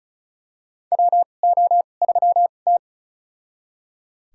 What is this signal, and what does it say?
Morse code, heard as a keyed tone, about 36 wpm, spaced as for 25 wpm, 700 Hz. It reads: WO3T
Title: USA Call Signs - 4 Characters - 2x Character Spacing - 35wpm